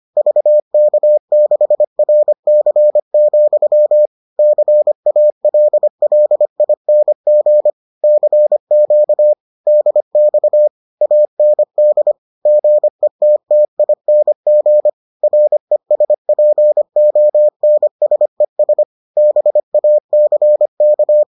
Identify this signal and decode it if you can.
Morse code, heard as a keyed tone, about 25 wpm, 600 Hz. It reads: VK6RC, CALLING CQ DX AND GETTING RESPONSES BACK